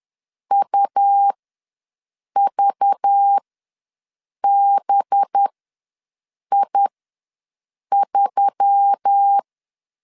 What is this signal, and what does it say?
Morse code, heard as a keyed tone, about 10 wpm, spaced as for 6 wpm, 800 Hz. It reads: UVBI3
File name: LEZIONI / 5 LBVC / 5 LBVC ws 10wpm 3733